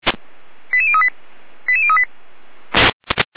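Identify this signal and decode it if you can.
Five-tone selective call (ZVEI), heard as quick runs of five tones, tones of about 70 ms. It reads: 88038 88038
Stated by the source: ZVEI five-tone selective call received on SDR, from audio.f85360.feuerwehr-orlamuende.org fire brigade archive.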